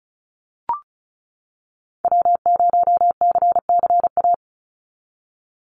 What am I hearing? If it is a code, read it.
Morse code, heard as a keyed tone, about 36 wpm, 700 Hz. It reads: W0CCA
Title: NA SOTA Activator Callsigns 35wpm